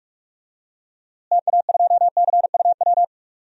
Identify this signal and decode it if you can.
Morse code, heard as a keyed tone, about 46 wpm, 700 Hz. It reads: TA2CUW
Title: DXCC-100 Call Signs - 6 Characters - Repeated 2x - 45wpm